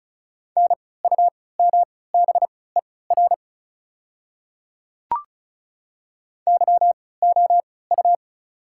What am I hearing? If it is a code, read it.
Morse code, heard as a keyed tone, about 36 wpm, spaced as for 20 wpm, 690 Hz. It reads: NUMBER YOU
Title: 3x Character Spacing - Top 100 Words - 35wpm